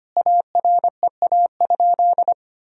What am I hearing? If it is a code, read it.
Morse code, heard as a keyed tone, about 25 wpm, 700 Hz. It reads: AREA?
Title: Sentences from Top 1000 Words - Review Part 2 - 25wpm